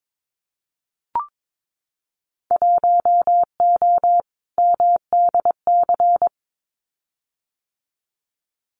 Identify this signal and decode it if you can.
Morse code, heard as a keyed tone, about 22 wpm, 700 Hz. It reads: 1O MDC
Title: Winter Field Day Exchanges 22wpm